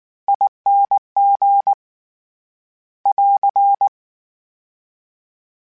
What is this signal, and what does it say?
Morse code, heard as a keyed tone, about 19 wpm, 800 Hz. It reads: ING <AR>
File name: Special INGs / Special INGs-mp3s / SlowToFaster ing @18